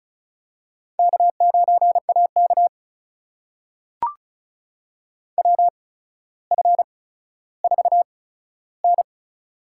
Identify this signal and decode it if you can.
Morse code, heard as a keyed tone, about 34 wpm, spaced as for 15 wpm, 700 Hz. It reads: K9AK WF4N
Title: USA Call Signs - 4 Characters - 8x Character Spacing - 35wpm